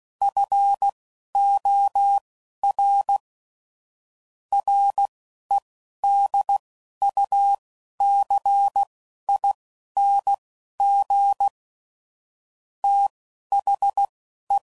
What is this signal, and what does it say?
Morse code, heard as a keyed tone, about 16 wpm, spaced as for 11 wpm, 800 Hz. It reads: FOR REDUCING THE
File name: CW Code Groups @15WPM / QSOs / WD0FBT Spaced